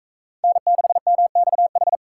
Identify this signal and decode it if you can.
Morse code, heard as a keyed tone, about 40 wpm, 700 Hz. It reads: N6MXH